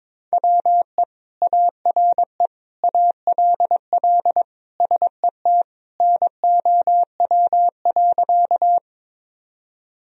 Morse code, heard as a keyed tone, about 22 wpm, 700 Hz. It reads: WE ARE ALL SET NOW.
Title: Sentences from Top 1000 Words - Review Part 1 - 22wpm